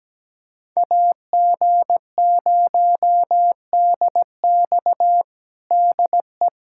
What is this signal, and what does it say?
Morse code, heard as a keyed tone, about 17 wpm, 700 Hz. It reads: AG0DX DE
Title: USA Callsign DE Callsign - 5 Characters - 17wpm